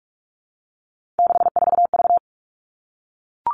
Morse code, heard as a keyed tone, about 44 wpm, 700 Hz. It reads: 64V
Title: All Cut Numbers - 3-Digit 45wpm